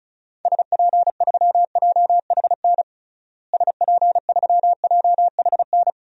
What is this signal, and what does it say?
Morse code, heard as a keyed tone, about 34 wpm, 700 Hz. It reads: SP3JHN SP3JHN
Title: DXCC-100 Call Signs - 6 Characters - Repeated 2x - 35wpm